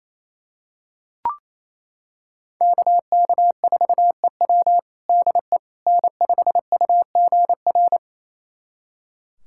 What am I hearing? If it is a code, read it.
Morse code, heard as a keyed tone, about 28 wpm, 700 Hz. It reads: KK4EW DE N5UGR
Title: USA Callsign DE Callsign - 5 Characters - 28wpm